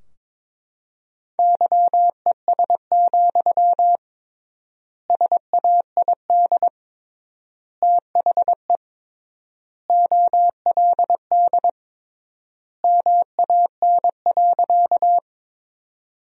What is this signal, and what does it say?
Morse code, heard as a keyed tone, about 22 wpm, 700 Hz. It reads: YES, SAID THE OLD MAN.
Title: 3x Word-Spacing - Sentences from Top 500 Words - 22wpm